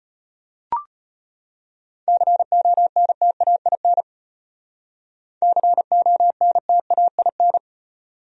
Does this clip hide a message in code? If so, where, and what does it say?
Yes: Morse code from 2.1 to 7.6 s, heard as a keyed tone, about 33 wpm, 700 Hz. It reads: CONTAIN CONTAIN